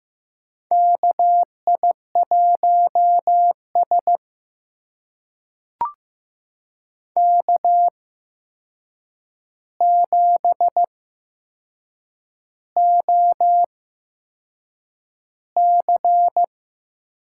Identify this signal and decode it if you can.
Morse code, heard as a keyed tone, about 15 wpm, spaced as for 6 wpm, 700 Hz. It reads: KI1S K7OC
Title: USA Call Signs - 4 Characters - 8x Character Spacing - 15wpm